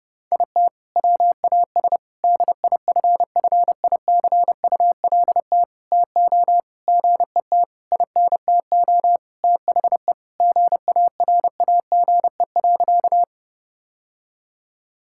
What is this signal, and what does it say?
Morse code, heard as a keyed tone, about 30 wpm, 700 Hz. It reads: IT WAS DIFFICULT TO GET INTO THE GARAGE.
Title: Sentences from Top 1600 Words 30wpm